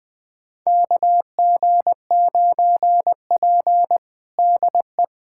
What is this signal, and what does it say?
Morse code, heard as a keyed tone, about 20 wpm, 700 Hz. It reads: KG9P DE
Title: USA Callsign DE Callsign - 4 Characters - 20wpm